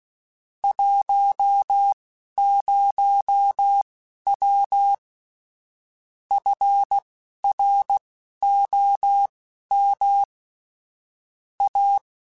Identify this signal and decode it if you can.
Morse code, heard as a keyed tone, about 16 wpm, spaced as for 11 wpm, 800 Hz. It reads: 10W FROM A
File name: QSOs / QSOs-mp3s / kc7haa@10